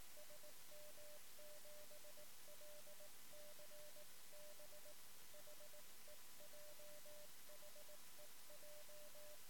Morse code, heard as a keyed tone, about 18 wpm, 600 Hz. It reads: SM7LCB HEJ HEJ
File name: sm7lcb / 1 oh2pu-p 13cm